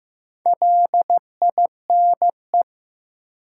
Morse code, heard as a keyed tone, about 15 wpm, 700 Hz. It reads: LINE
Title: Sentences from Top 1200 Words - New Word Drill Alt - 15wpm